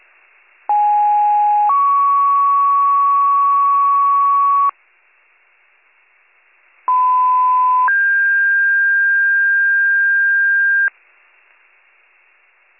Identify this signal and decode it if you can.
Two-tone page, heard as a one-second tone, then a three-second tone, about 830 then 1120 Hz, about 1010 then 1600 Hz. It is heard from 0.7 to 10.9 s.